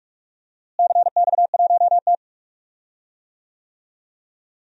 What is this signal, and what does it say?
Morse code, heard as a keyed tone, about 41 wpm, 700 Hz. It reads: CX1T
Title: DXCC-100 Call Signs - 4 Characters - Repeated 2x - 45wpm